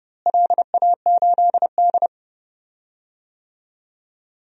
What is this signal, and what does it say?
Morse code, heard as a keyed tone, about 31 wpm, 700 Hz. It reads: LA8D
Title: DXCC-100 Call Signs - 4 Characters - Repeated 2x - 30wpm